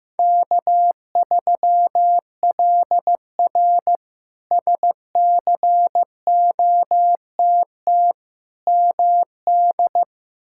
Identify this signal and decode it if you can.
Morse code, heard as a keyed tone, about 15 wpm, 700 Hz. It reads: K3LR SCOTT MD
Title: K1USN SST QSO Practice - 15wpm